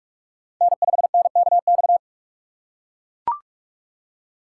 Morse code, heard as a keyed tone, about 42 wpm, 700 Hz. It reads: N5NKX